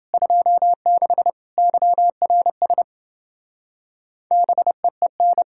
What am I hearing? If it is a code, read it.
Morse code, heard as a keyed tone, about 30 wpm, 700 Hz. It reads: 26 YRS BEEN